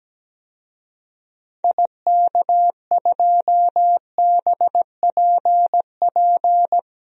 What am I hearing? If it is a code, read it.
Morse code, heard as a keyed tone, about 17 wpm, 700 Hz. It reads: IK2BPP
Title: DXCC-100 Call Signs - 6 Characters - 17wpm